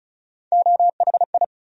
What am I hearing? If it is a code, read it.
Morse code, heard as a keyed tone, about 36 wpm, 700 Hz. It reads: OHI
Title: USA Call Sign Postfixes - 3 Letters 35wpm